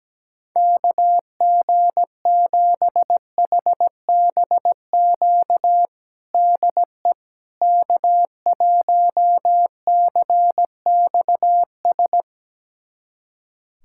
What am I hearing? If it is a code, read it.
Morse code, heard as a keyed tone, about 17 wpm, 700 Hz. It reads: KG7HBQ DE K1CXS
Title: USA Callsign DE Callsign 17wpm